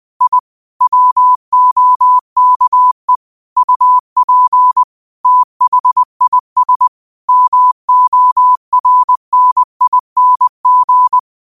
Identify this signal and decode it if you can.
Morse code, heard as a keyed tone, about 20 wpm, 1000 Hz. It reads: I WOKE UP THIS MORNING
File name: qotdfiles / 20wpm